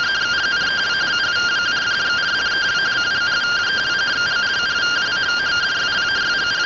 Two-tone teleprinter signal, about 170 Hz shift.
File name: findings / MR AXIAL 0mm 6mm.gen-16 rtty rx